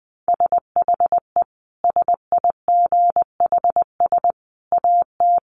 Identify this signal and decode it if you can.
Morse code, heard as a keyed tone, about 20 wpm, 700 Hz. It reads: SHE SIGHS AT
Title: Dit Challenge 20wpm